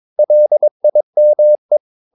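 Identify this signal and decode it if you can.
Morse code, heard as a keyed tone, about 22 wpm, 600 Hz. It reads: LIME
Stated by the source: Morse practice of short Danish words at 22 wpm, from CW med Gnister.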